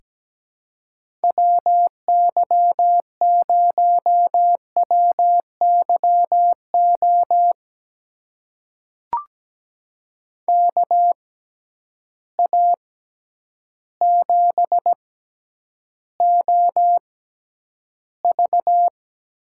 Morse code, heard as a keyed tone, about 17 wpm, spaced as for 9 wpm, 700 Hz. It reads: WY0WYO KA7OV
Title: USA Call Signs - 6 Characters - 6x Character Spacing - 17wpm